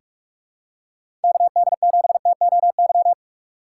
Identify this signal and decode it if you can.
Morse code, heard as a keyed tone, about 44 wpm, 700 Hz. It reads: KD7TOY
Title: USA Call Signs - 4x Character Spacing - 45wpm